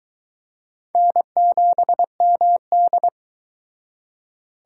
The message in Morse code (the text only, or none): N7MD